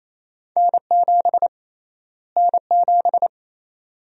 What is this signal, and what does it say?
Morse code, heard as a keyed tone, about 28 wpm, 700 Hz. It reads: N7 N7
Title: USA Call Sign Prefixes - Repeated 2x - 28wpm